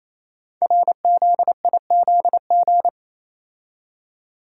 Morse code, heard as a keyed tone, about 27 wpm, 700 Hz. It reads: RZIZG
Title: Code Groups - 5 Letters - 28wpm